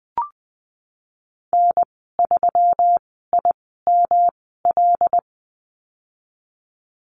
Morse code, heard as a keyed tone, about 20 wpm, spaced as for 15 wpm, 700 Hz. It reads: N3IML
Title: 128 Lesson 21g - Number 6 - Call Signs 20wpm